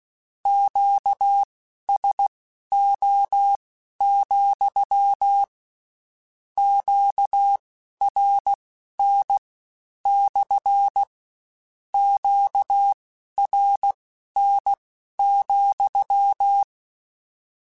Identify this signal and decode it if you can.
Morse code, heard as a keyed tone, about 16 wpm, spaced as for 11 wpm, 800 Hz. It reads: QSO, QRN / QRN,
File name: Q Calls / Q Calls-mp3s / Q Calls SPSL2 @15